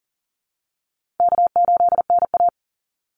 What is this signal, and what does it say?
Morse code, heard as a keyed tone, about 40 wpm, 700 Hz. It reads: K8NA
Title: Easy Call Signs Repeated 2x - 40wpm